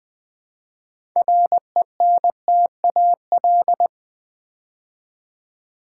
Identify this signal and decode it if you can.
Morse code, heard as a keyed tone, about 20 wpm, 700 Hz. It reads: RENTAL
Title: Build A Word - 2 Letters At A Time - 6-Letter Words - 20wpm